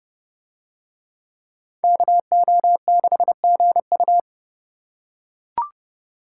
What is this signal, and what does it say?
Morse code, heard as a keyed tone, about 28 wpm, 700 Hz. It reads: KO6GU